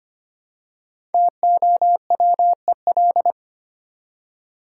Morse code, heard as a keyed tone, about 25 wpm, 700 Hz. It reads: TOWEL
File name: practice / 25wpm / Easy Words with Hints - 25wpm